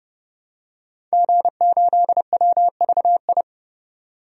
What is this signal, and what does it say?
Morse code, heard as a keyed tone, about 30 wpm, 700 Hz. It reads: G8WVI
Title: DXCC-100 Call Signs - 5 Characters - 30wpm